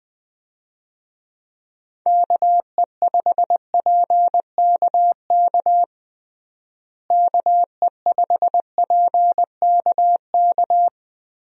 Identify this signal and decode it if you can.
Morse code, heard as a keyed tone, about 20 wpm, 700 Hz. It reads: KE5PKK KE5PKK
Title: USA Call Signs - 6 Characters - Repeated 2x - 20wpm